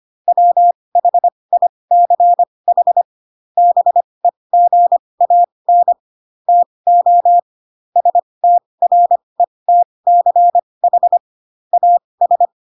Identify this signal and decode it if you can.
Morse code, heard as a keyed tone, about 25 wpm, spaced as for 20 wpm, 700 Hz. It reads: WHICH BEGAN TO STRETCH AS